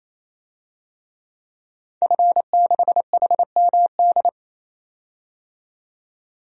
Morse code, heard as a keyed tone, about 28 wpm, 700 Hz. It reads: F6HMD
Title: DXCC-100 Call Signs - 5 Characters - 28wpm